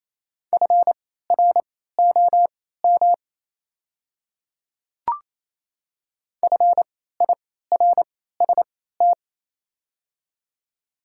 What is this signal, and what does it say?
Morse code, heard as a keyed tone, about 28 wpm, spaced as for 16 wpm, 690 Hz. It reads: FROM FIRST
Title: 3x Character Spacing - Top 100 Words - 28wpm